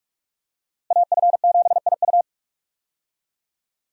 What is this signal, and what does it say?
Morse code, heard as a keyed tone, about 46 wpm, 700 Hz. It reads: AF7IU
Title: USA Call Signs - 5 Characters - Repeated 2x - 45wpm